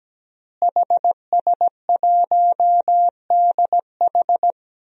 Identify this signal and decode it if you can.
Morse code, heard as a keyed tone, about 17 wpm, 700 Hz. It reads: HS1DH
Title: DXCC-100 Call Signs - 5 Characters - Repeated 2x - 17wpm